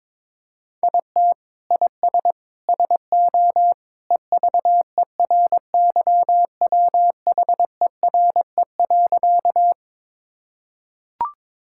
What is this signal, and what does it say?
Morse code, heard as a keyed tone, about 21 wpm, 700 Hz. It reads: IT IS SO EVERYWHERE.
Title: Sentences from Top 1600 Words 22wpm